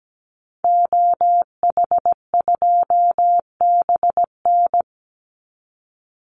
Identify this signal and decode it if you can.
Morse code, heard as a keyed tone, about 17 wpm, 700 Hz. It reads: OH2BN
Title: Call Signs Repeated 2x - 17wpm